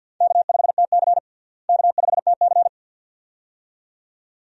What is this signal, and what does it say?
Morse code, heard as a keyed tone, about 52 wpm, 700 Hz. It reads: K5TC K5TC
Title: USA Call Signs - 4 Characters - Repeated 2x - 50wpm